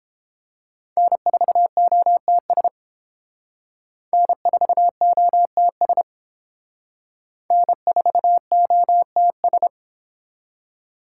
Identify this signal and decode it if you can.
Morse code, heard as a keyed tone, about 29 wpm, 700 Hz. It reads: N4OTS N4OTS N4OTS